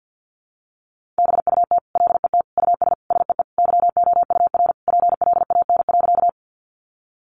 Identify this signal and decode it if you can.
Morse code, heard as a keyed tone, about 50 wpm, 700 Hz. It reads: BUT LET US SEE YOUR PLAN.
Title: Sentences from Top 500 Words 50wpm